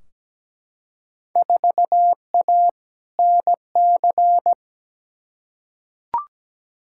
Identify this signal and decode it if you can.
Morse code, heard as a keyed tone, about 17 wpm, 700 Hz. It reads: 4A NC
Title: Field Day Exchanges - 17wpm